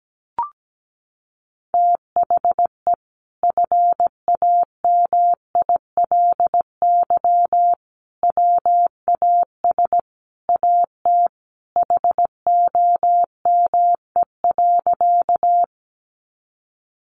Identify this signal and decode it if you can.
Morse code, heard as a keyed tone, about 17 wpm, 700 Hz. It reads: THE FAMILY WAS AT HOME.
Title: Sentences from Top 400 Words 17wpm